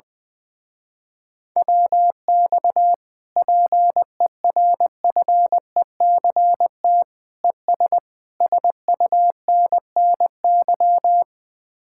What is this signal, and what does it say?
Morse code, heard as a keyed tone, about 20 wpm, 700 Hz. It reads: WX PERFECT ES SUNNY
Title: Speed-Racing Common QSO Phrases - 20wpm